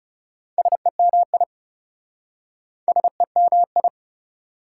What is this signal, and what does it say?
Morse code, heard as a keyed tone, about 38 wpm, 700 Hz. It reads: SEMI SEMI